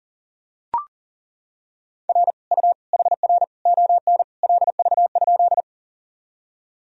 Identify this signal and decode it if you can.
Morse code, heard as a keyed tone, about 40 wpm, 700 Hz. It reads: R U HR ON LV?
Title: Sentences from Top 400 Words and Abbreviations 40wpm